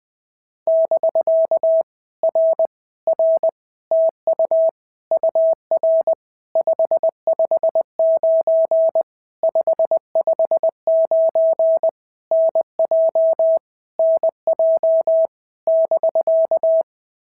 Morse code, heard as a keyed tone, about 20 wpm, 650 Hz. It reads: <BK> R R TU UR 559 559 NJ NJ <BK>
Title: POTA QSO Practice - 20wpm